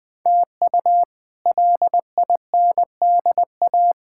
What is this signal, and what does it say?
Morse code, heard as a keyed tone, about 21 wpm, 700 Hz. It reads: TU LINDA
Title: K1USN SST QSO Practice - 20wpm